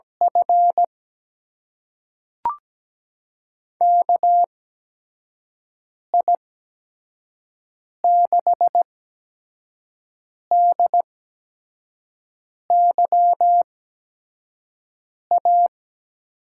Morse code, heard as a keyed tone, about 17 wpm, spaced as for 5 wpm, 700 Hz. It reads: F KI6DYA
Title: USA Call Signs - 6 Characters - 8x Character Spacing - 17wpm